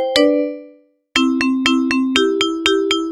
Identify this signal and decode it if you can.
There is no code.